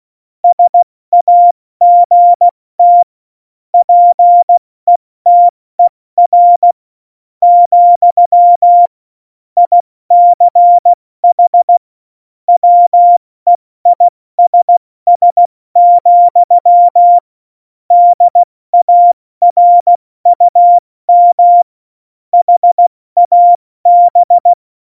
Morse code, heard as a keyed tone, about 16 wpm, spaced as for 14 wpm, 700 Hz. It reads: SAGT PETER, ICH WEISS, DARUM HAB
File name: Morsekurs / Witze / Witz 12. Februar 2019, 70 F80